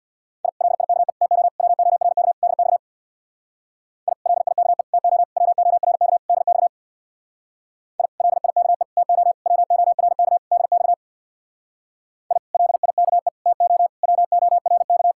Morse code, heard as a keyed tone, about 65 wpm, 700 Hz. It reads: I LIKE TO WORK DX I LIKE TO WORK DX I LIKE TO WORK DX I LIKE TO WORK